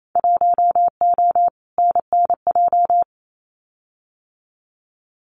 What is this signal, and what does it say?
Morse code, heard as a keyed tone, about 28 wpm, 700 Hz. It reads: 1O NNJ
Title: Winter Field Day Exchanges 28wpm